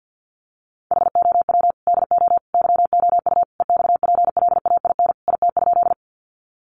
Morse code, heard as a keyed tone, about 48 wpm, 700 Hz. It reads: HOW DO YOU EXPLAIN IT?
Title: Sentences from Top 1100 Words 50wpm